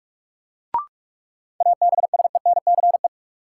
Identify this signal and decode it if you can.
Morse code, heard as a keyed tone, about 42 wpm, 700 Hz. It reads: ABSENCE
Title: Sets of 4 Words - New Word Drill - 45wpm